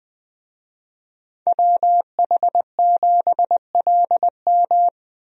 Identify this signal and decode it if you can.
Morse code, heard as a keyed tone, about 21 wpm, 700 Hz. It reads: WH7LM
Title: USA Call Signs - 5 Characters - Repeated 3x - 20wpm